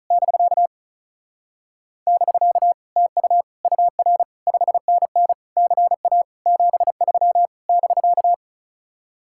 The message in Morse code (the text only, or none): <BK> <BK> TU UR 5NN CA 73 <BK>